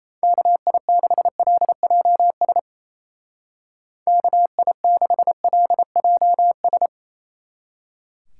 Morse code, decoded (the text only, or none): KI6LJS KI6LJS